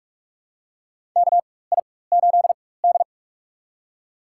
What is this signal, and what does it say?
Morse code, heard as a keyed tone, about 46 wpm, spaced as for 21 wpm, 700 Hz. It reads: KI8D
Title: USA Call Signs - 4 Characters - 4x Character Spacing - 45wpm